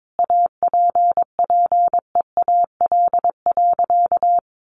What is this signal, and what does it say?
Morse code, heard as a keyed tone, about 22 wpm, 700 Hz. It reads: APPEAL.
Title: Sentences from Top 1300 Words 22wpm